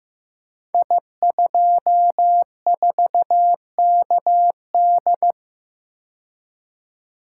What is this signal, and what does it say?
Morse code, heard as a keyed tone, about 15 wpm, 700 Hz. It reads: I24KD
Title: Code Groups - 5 Letter:Numbers - 15wpm